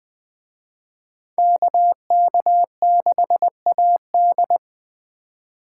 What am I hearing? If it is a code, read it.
Morse code, heard as a keyed tone, about 20 wpm, 700 Hz. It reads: KK6AD
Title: Speed Racing - USA Call Signs 17wpm